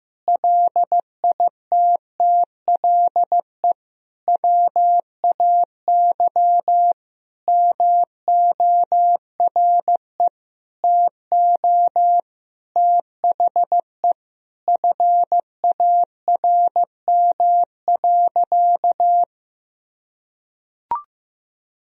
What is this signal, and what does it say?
Morse code, heard as a keyed tone, about 15 wpm, 700 Hz. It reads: LITTLE WAY MORE TO THE FARM.
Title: Sentences from Top 1000 Words - Review Part 2 - 15wpm